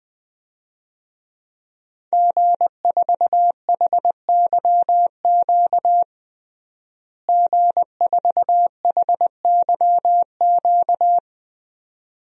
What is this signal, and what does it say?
Morse code, heard as a keyed tone, about 20 wpm, 700 Hz. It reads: G4HYQ G4HYQ
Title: DXCC-100 Call Signs - 5 Characters - Repeated 2x - 20wpm